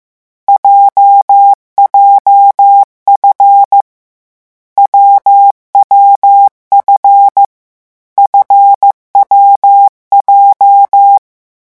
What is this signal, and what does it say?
Morse code, heard as a keyed tone, about 15 wpm, 800 Hz. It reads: JJF WWF FWJ
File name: morse mp3 / lecon7-vitesse 14